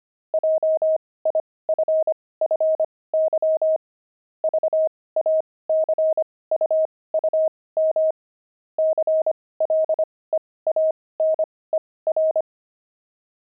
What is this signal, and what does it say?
Morse code, heard as a keyed tone, about 25 wpm, spaced as for 18 wpm, 630 Hz. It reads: JIFFY VACUUM CLEANER
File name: pangram-fs-18-cs-25-aardvark